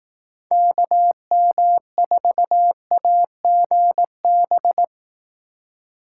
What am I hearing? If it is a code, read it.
Morse code, heard as a keyed tone, about 18 wpm, 700 Hz. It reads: KM4AGB
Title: Speed Racing - USA 2x3 Call Signs 15wpm